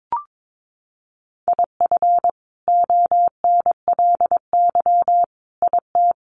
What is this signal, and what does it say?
Morse code, heard as a keyed tone, about 22 wpm, 700 Hz. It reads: IF ONLY IT
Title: Sentences from Top 500 Words 22wpm